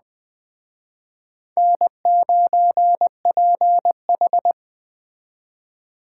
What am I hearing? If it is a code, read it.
Morse code, heard as a keyed tone, about 20 wpm, 700 Hz. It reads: N9PH
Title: Speed Racing - USA 1x2 Call Signs 17wpm